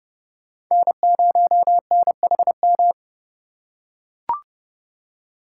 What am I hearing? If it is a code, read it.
Morse code, heard as a keyed tone, about 29 wpm, 700 Hz. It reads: N0NHM